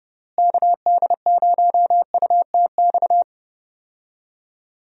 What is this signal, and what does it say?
Morse code, heard as a keyed tone, about 30 wpm, 700 Hz. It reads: KD0UTX
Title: USA Call Signs - 6 Characters - Repeated 2x - 30wpm